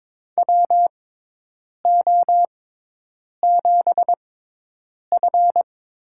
Morse code, heard as a keyed tone, about 22 wpm, spaced as for 8 wpm, 700 Hz. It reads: WO7F
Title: USA Call Signs - 6x Character Spacing - 22wpm